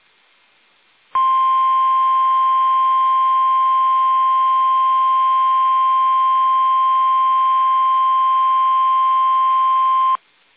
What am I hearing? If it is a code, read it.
Emergency Alert System (SAME) tones (the 1050 Hz attention tone). It is heard from 1.1 to 10.2 s.
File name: eas / KUSO / KUSO